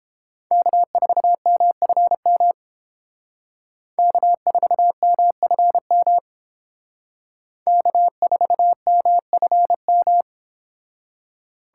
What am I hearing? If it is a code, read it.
Morse code, heard as a keyed tone, about 30 wpm, 700 Hz. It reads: K4MFM K4MFM K4MFM